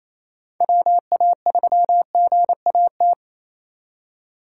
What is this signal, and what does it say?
Morse code, heard as a keyed tone, about 32 wpm, 700 Hz. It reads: WA3GAT
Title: Speed Racing - USA 2x3 Call Signs 28wpm